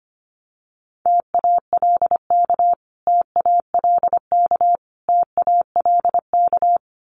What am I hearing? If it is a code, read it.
Morse code, heard as a keyed tone, about 25 wpm, 700 Hz. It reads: TALK TALK TALK
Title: Top 500 Words Repeated 25wpm